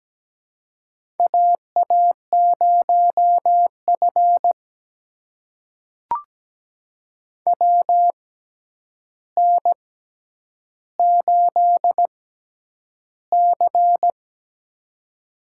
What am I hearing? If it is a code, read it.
Morse code, heard as a keyed tone, about 17 wpm, spaced as for 9 wpm, 700 Hz. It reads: AA0F WN8C
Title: USA Call Signs - 4 Characters - 6x Character Spacing - 17wpm